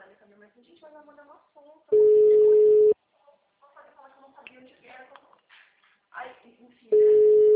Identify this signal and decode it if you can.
Ringback tone (Europe, 425 Hz), with speech.
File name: out- -2016- - - .2